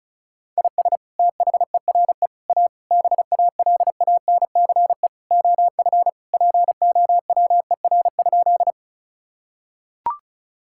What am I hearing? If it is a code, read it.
Morse code, heard as a keyed tone, about 36 wpm, 700 Hz. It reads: IS THERE A BALANCE OF POWER?
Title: Sentences from Top 1000 Words - Review Part 6 - 35wpm